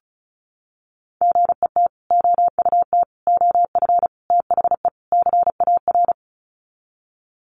Morse code, heard as a keyed tone, about 34 wpm, 700 Hz. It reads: GET OUT OF THE CAR
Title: Sets of 5 Words Encore2 35wpm